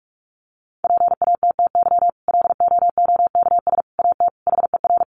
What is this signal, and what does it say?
Morse code, heard as a keyed tone, about 44 wpm, 700 Hz. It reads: PATTY LOOKS AT HER
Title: Additional Easy Phrases 45wpm